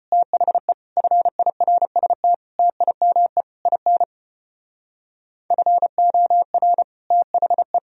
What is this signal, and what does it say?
Morse code, heard as a keyed tone, about 35 wpm, 700 Hz. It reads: THE FIRST TIME IN FOR THE